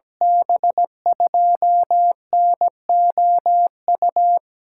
Morse code, heard as a keyed tone, about 17 wpm, 700 Hz. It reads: B2NOU
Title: USA Call Signs - 6 Characters - 8x Character Spacing - 17wpm